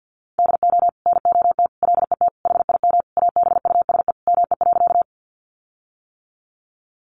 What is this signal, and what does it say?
Morse code, heard as a keyed tone, about 50 wpm, 700 Hz. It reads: DO NOT LET HIM ABUSE ME.
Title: Sentences from Top 1300 Words 50wpm